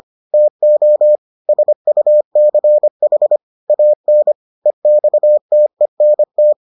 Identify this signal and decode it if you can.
Morse code, heard as a keyed tone, about 25 wpm, 600 Hz. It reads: TO SUCH AN EXTENT